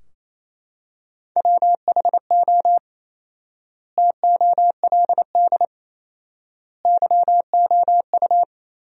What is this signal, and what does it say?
Morse code, heard as a keyed tone, about 28 wpm, 700 Hz. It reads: WHO TOLD YOU
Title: 4x Word-Spacing - Sentences from Top 400 Words - 28wpm